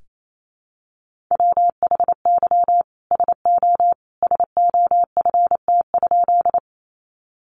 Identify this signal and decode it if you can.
Morse code, heard as a keyed tone, about 28 wpm, 700 Hz. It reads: WHY SO SOFT?
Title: Sentences from Top 1500 Words 28wpm